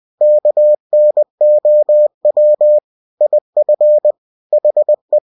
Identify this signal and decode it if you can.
Morse code, heard as a keyed tone, about 20 wpm, 600 Hz. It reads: KNOW IF HE
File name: cw-books / winnie-the-pooh / winnie-20wpm-page-0104